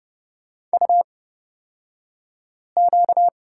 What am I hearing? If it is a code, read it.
Morse code, heard as a keyed tone, about 30 wpm, spaced as for 5 wpm, 700 Hz. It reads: UQ
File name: Lesson 38d - Letter q - Two Characters 30wpm